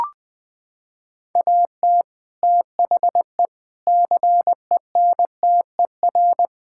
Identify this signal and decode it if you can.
Morse code, heard as a keyed tone, about 20 wpm, 700 Hz. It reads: AT THE CENTER
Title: Sentences with 4-Word Phrases - 20wpm